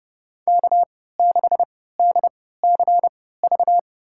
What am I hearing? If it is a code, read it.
Morse code, heard as a keyed tone, about 30 wpm, spaced as for 17 wpm, 700 Hz. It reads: K6DCV
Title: USA Call Signs - 3x Character Spacing - 30wpm